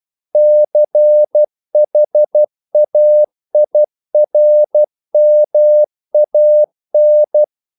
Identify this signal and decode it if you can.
Morse code, heard as a keyed tone, about 12 wpm, 600 Hz. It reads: CHAIRMAN
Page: English quotations in CW (telegraphy - morse)